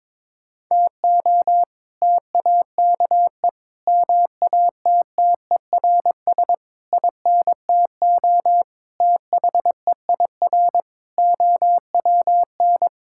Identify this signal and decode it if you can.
Morse code, heard as a keyed tone, about 22 wpm, 700 Hz. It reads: TO TAKE MATTERS INTO THEIR OWN
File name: practice / 22wpm / Sets of 7 Words - 22wpm